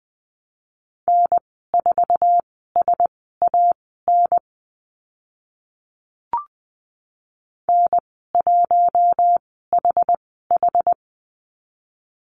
Morse code, heard as a keyed tone, about 20 wpm, spaced as for 14 wpm, 700 Hz. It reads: N4SAN N1HH
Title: 062 Lesson 11g - Letter h - Call Signs 20wpm